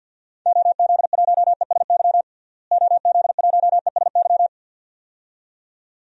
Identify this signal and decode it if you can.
Morse code, heard as a keyed tone, about 51 wpm, 700 Hz. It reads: OZ1ESY OZ1ESY
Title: DXCC-100 Call Signs - 6 Characters - Repeated 2x - 50wpm